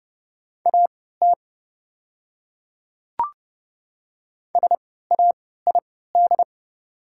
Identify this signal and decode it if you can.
Morse code, heard as a keyed tone, about 29 wpm, spaced as for 17 wpm, 690 Hz. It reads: AT SAID
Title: 3x Character Spacing - Top 300 Words - 30wpm